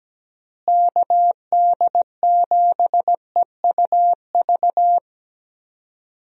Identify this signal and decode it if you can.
Morse code, heard as a keyed tone, about 17 wpm, 700 Hz. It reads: KD7EUV
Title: USA Call Signs - 6 Characters - Repeated 3x - 17wpm